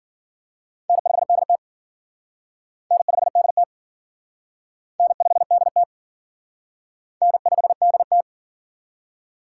Morse code, heard as a keyed tone, about 52 wpm, 700 Hz. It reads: N5DT N5DT N5DT N5DT